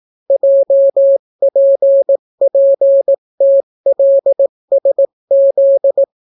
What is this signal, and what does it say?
Morse code, heard as a keyed tone, about 18 wpm, 550 Hz. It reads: JPPTLSZ